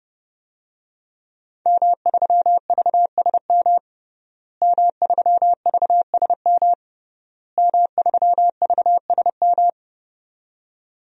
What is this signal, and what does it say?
Morse code, heard as a keyed tone, about 30 wpm, 700 Hz. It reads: M3VSM M3VSM M3VSM
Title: DXCC-100 Call Signs - 5 Characters - Repeated 3x - 30wpm